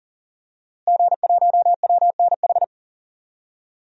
Morse code, heard as a keyed tone, about 40 wpm, 700 Hz. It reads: G1WNH
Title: DXCC-100 Call Signs - 5 Characters - Repeated 3x - 40wpm